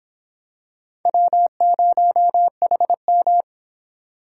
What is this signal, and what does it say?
Morse code, heard as a keyed tone, about 26 wpm, 700 Hz. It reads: W0HM